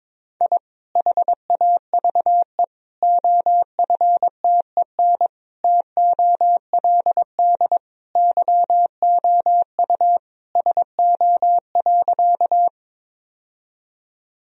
Morse code, heard as a keyed tone, about 22 wpm, 700 Hz. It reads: I HAVE OFTEN TOLD YOU SO.
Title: Sentences from Top 1000 Words - Review Part 1 - 22wpm